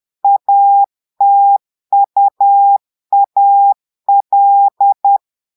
Morse code, heard as a keyed tone, about 10 wpm, 800 Hz. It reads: ATUAL
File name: Portugais - texte en clair 1 - vitesse - 10 mots minute